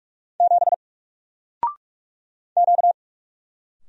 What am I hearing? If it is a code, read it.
Morse code, heard as a keyed tone, about 44 wpm, 700 Hz. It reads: 7 Q
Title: Single Letter-Number 45wpm